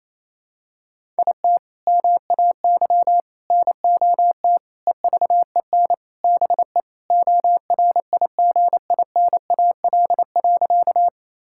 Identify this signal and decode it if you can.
Morse code, heard as a keyed tone, about 28 wpm, 700 Hz. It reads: IT MAY NOT EVEN BE ORIGINAL.
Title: Sentences from Top 1000 Words - Review Part 6 - 28wpm